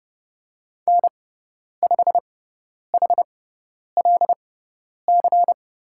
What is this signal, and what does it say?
Morse code, heard as a keyed tone, about 31 wpm, spaced as for 10 wpm, 700 Hz. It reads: N5HLC
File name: Lesson 16g - Letter c - Call Signs 30wpm